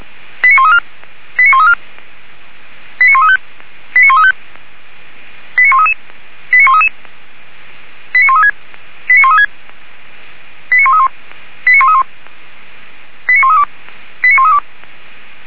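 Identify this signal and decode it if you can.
Five-tone selective call (ZVEI), heard as quick runs of five tones, tones of about 70 ms. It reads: 89134 89134 89135 89135 89133 89133 89136 89136 89131 89131 89132 89132
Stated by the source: ZVEI five-tone call received by SDR.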